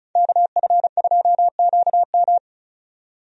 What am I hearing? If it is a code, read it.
Morse code, heard as a keyed tone, about 36 wpm, 700 Hz. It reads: KF2QM
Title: USA Call Signs - 5 Characters - 35wpm